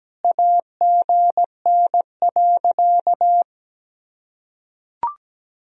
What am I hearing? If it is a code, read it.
Morse code, heard as a keyed tone, about 17 wpm, 700 Hz. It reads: AGN.
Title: Sentences from Top 500 Words and Abbreviations 17wpm